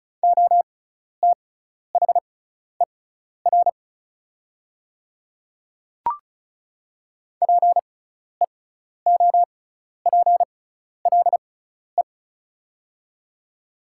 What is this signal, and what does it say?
Morse code, heard as a keyed tone, about 36 wpm, spaced as for 12 wpm, 690 Hz. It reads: OTHER PEOPLE
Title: 6x Character Spacing - Top 100 Words - 35wpm